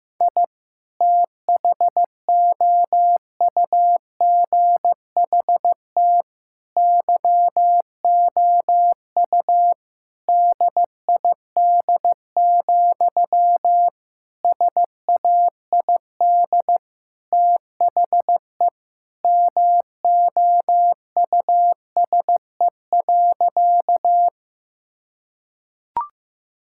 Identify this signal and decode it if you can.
Morse code, heard as a keyed tone, about 15 wpm, 700 Hz. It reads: I THOUGHT YOU DID, SAID THE MOUSE.
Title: Sentences from Top 1000 Words - Review Part 5 - 15wpm